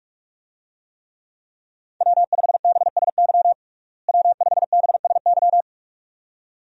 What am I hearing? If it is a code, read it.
Morse code, heard as a keyed tone, about 45 wpm, 700 Hz. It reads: W5BSY W5BSY
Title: USA Call Signs - 5 Characters - Repeated 2x - 45wpm